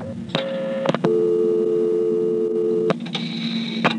Dial tone (North America), 350 + 440 Hz.